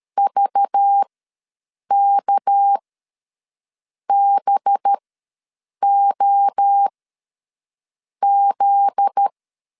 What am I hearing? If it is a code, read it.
Morse code, heard as a keyed tone, about 13 wpm, spaced as for 6 wpm, 800 Hz. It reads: VKBOZ
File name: LEZIONI / 7 GF9Z / 7 GF9Z ws 12wpm 3321